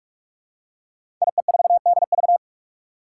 Morse code, heard as a keyed tone, about 44 wpm, 700 Hz. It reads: IE4DV
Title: Five Challenging Letter-Number Combinations 45wpm